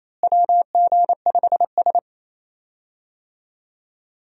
Morse code, heard as a keyed tone, about 28 wpm, 700 Hz. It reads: WG5S
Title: USA Call Signs - 4 Characters - Repeated 2x - 28wpm